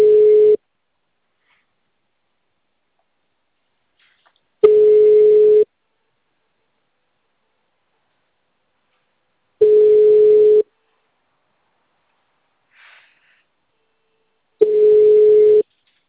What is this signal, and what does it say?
Ringback tone (Europe, 425 Hz).